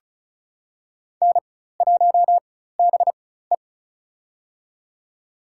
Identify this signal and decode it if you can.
Morse code, heard as a keyed tone, about 36 wpm, spaced as for 16 wpm, 700 Hz. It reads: N1BE
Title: USA Call Signs - 4 Characters - 4x Character Spacing - 35wpm